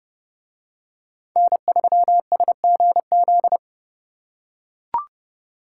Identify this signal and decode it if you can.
Morse code, heard as a keyed tone, about 28 wpm, 700 Hz. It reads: N3SGZ